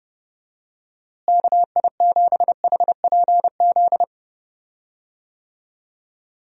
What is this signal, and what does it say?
Morse code, heard as a keyed tone, about 31 wpm, 700 Hz. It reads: KI7HPZ
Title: USA Call Signs - 6 Characters - Repeated 2x - 30wpm